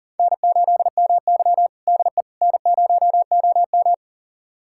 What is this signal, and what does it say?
Morse code, heard as a keyed tone, about 40 wpm, 700 Hz. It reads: N8MY DE N0OM
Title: USA Callsign DE Callsign - 4 Characters - 40wpm